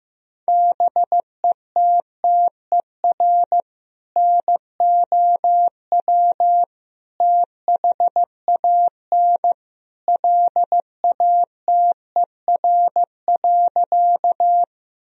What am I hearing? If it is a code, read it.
Morse code, heard as a keyed tone, about 15 wpm, 700 Hz. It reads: BETTER NOW THAN LATER.
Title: Sentences from Top 1000 Words - Review Part 4 - 15wpm